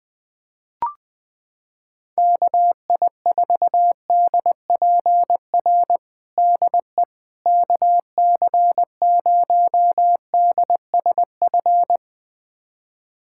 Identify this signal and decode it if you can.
Morse code, heard as a keyed tone, about 21 wpm, 700 Hz. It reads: KI4DPR DE KC0DSF
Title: USA Callsign DE Callsign 20wpm